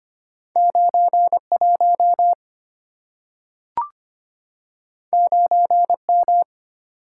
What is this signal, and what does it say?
Morse code, heard as a keyed tone, about 25 wpm, 700 Hz. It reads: 91 9M
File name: Two Letter-Number Combinations - 25wpm